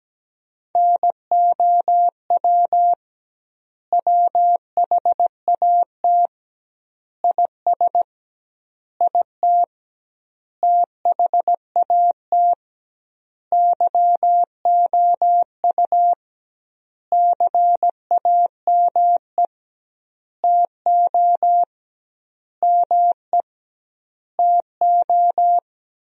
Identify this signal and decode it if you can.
Morse code, heard as a keyed tone, about 17 wpm, 700 Hz. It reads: NOW WHAT IS IT THAT YOU CAME TO ME TO
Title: 2x Word-Spacing - Sentences from Top 200 Words - 17wpm